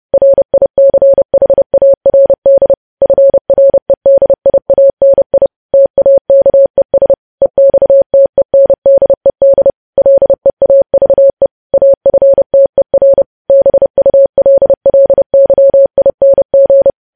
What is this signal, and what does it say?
Morse code, heard as a keyed tone, about 30 wpm, 550 Hz. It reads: RICHARD FREDIANI TAKES EXTENDED LEAVE AFTER BULLYING
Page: news headlines in morse code at 30 wpm